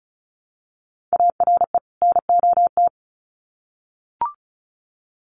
Morse code, heard as a keyed tone, about 34 wpm, 700 Hz. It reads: ARE NOT
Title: Sets of 2 Words 35wpm